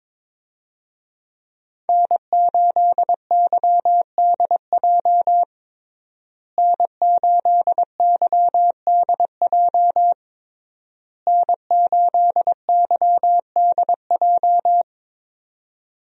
Morse code, heard as a keyed tone, about 22 wpm, 700 Hz. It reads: N8YDJ N8YDJ N8YDJ
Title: USA Call Signs - 5 Characters - Repeated 3x - 22wpm